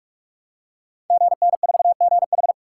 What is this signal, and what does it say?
Morse code, heard as a keyed tone, about 44 wpm, 700 Hz. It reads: GN4GH
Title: Code Groups - 5 Letter:Numbers - 45wpm